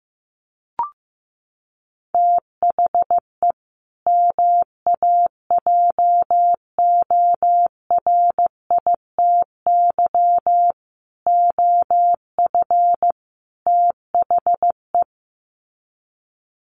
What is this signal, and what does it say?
Morse code, heard as a keyed tone, about 15 wpm, 700 Hz. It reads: THE MAJORITY OF THE
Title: Sets of 4 Words 15wpm